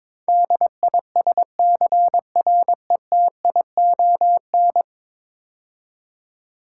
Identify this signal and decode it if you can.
Morse code, heard as a keyed tone, about 21 wpm, 700 Hz. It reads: DISCRETION
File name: practice / 22wpm / 10 Letter Words - First 3 Letters Given - 22wpm